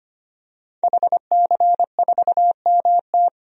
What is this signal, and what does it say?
Morse code, heard as a keyed tone, about 25 wpm, 700 Hz. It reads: HC4MT
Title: DXCC-100 Call Signs - 5 Characters - 25wpm